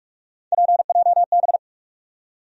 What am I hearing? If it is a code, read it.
Morse code, heard as a keyed tone, about 46 wpm, 700 Hz. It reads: PJB
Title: USA Call Sign Postfixes - 3 Letters 45wpm